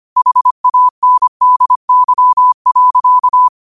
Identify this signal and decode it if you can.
Morse code, heard as a keyed tone, about 25 wpm, 1000 Hz. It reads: HANDY.